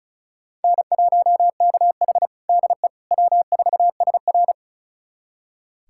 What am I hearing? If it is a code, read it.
Morse code, heard as a keyed tone, about 34 wpm, 700 Hz. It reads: N1KH DE W4SR
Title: USA Callsign DE Callsign - 4 Characters - 35wpm